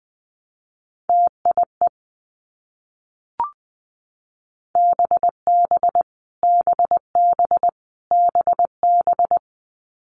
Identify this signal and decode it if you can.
Morse code, heard as a keyed tone, about 20 wpm, 700 Hz. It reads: TIE BB BB BB
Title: Multi-letter Phonemes Repeated 3x 20wpm